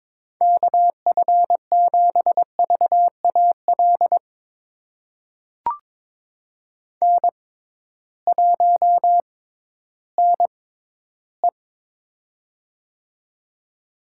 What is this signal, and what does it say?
Morse code, heard as a keyed tone, about 22 wpm, spaced as for 13 wpm, 700 Hz. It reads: KF7VAL N1NE
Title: USA Call Signs - 6x Character Spacing - 22wpm